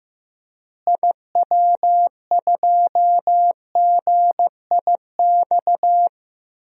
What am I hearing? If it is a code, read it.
Morse code, heard as a keyed tone, about 15 wpm, 700 Hz. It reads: IW2GIX
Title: DXCC-100 Call Signs - 6 Characters - Repeated 2x - 15wpm